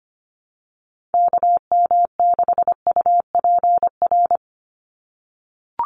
Morse code, heard as a keyed tone, about 25 wpm, 700 Hz. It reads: KM6UPR
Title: USA Call Signs 25wpm